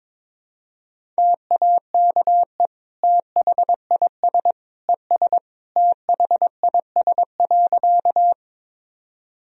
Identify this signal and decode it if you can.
Morse code, heard as a keyed tone, about 21 wpm, 700 Hz. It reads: TAKE THIS ES THIS.
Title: Sentences from Top 200 Words and Abbreviations 22wpm